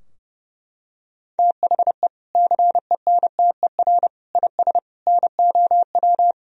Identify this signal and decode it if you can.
Morse code, heard as a keyed tone, about 30 wpm, 700 Hz. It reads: THE CENTER IS NOW
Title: Sentences from Top 1000 Words - Review Part 2 - 30wpm